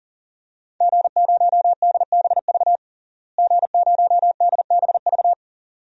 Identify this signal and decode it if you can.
Morse code, heard as a keyed tone, about 37 wpm, 700 Hz. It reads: G0DBV G0DBV
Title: DXCC-100 Call Signs - 5 Characters - Repeated 2x - 40wpm